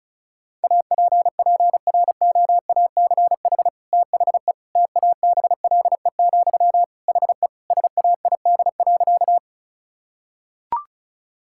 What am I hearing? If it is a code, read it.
Morse code, heard as a keyed tone, about 36 wpm, 700 Hz. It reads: APPROACH THE TABLE, HE SAID.
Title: Sentences from Top 1000 Words - Review Part 6 - 35wpm